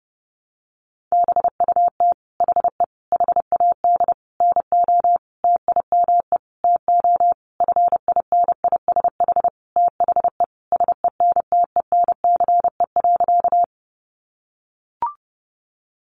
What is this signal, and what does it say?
Morse code, heard as a keyed tone, about 28 wpm, 700 Hz. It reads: BUT HE HAD NO TIME TO FINISH THE SENTENCE.
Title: Sentences from Top 1200 Words 30wpm